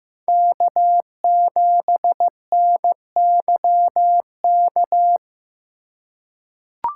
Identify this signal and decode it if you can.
Morse code, heard as a keyed tone, about 15 wpm, 700 Hz. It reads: K7NYK
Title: USA Call Signs - 2x Character Spacing - 15wpm